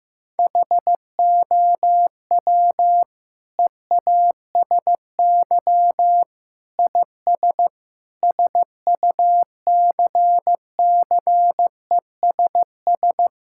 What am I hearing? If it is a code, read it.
Morse code, heard as a keyed tone, about 15 wpm, 700 Hz. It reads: HOW EASY IS SUCCESS
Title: Sentences from Top 1000 Words - Review Part 6 - 15wpm